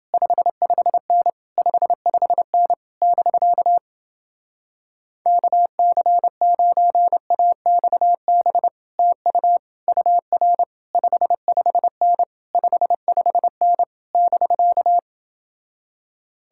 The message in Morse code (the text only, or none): 55N 55N <BK> KC9AXB TU UR 55N 55N <BK>